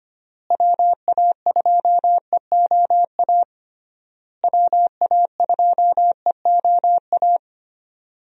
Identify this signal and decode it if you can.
Morse code, heard as a keyed tone, about 25 wpm, 700 Hz. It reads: WA2EOA WA2EOA
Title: USA Call Signs - 6 Characters - Repeated 3x - 25wpm